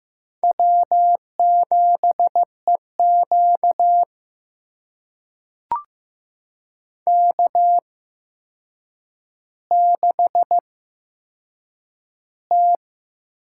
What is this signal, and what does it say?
Morse code, heard as a keyed tone, about 15 wpm, spaced as for 7 wpm, 700 Hz. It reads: W7EQ K6T
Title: USA Call Signs - 8x Character Spacing - 15wpm